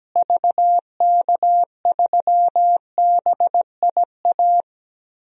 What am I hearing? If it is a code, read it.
Morse code, heard as a keyed tone, about 17 wpm, 700 Hz. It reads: VK3BIA
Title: DXCC-100 Call Signs - 6 Characters - Repeated 2x - 17wpm